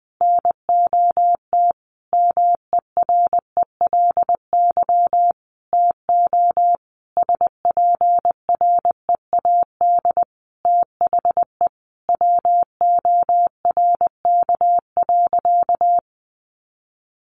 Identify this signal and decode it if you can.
Morse code, heard as a keyed tone, about 20 wpm, 700 Hz. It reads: NOT MERELY TO SPREAD THE WORK.
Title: Sentences from Top 1400 Words 20wpm